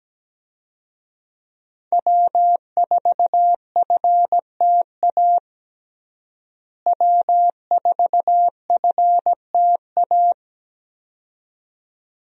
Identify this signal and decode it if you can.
Morse code, heard as a keyed tone, about 17 wpm, 700 Hz. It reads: W4FTA W4FTA
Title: USA Call Signs - 5 Characters - Repeated 2x - 17wpm